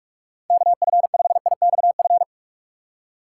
Morse code, heard as a keyed tone, about 44 wpm, 700 Hz. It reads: KF5IXF
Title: USA Call Signs - 6x Character Spacing - 45wpm